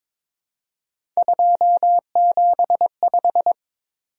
Morse code, heard as a keyed tone, about 22 wpm, 700 Hz. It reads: 275
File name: practice / 22wpm / Three Letter-Number Combinations - 22wpm